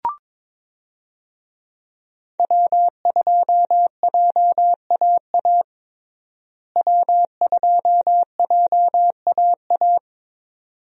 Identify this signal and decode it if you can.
Morse code, heard as a keyed tone, about 22 wpm, 700 Hz. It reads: W2JAA W2JAA
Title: USA Call Signs - 5 Characters - Repeated 2x - 22wpm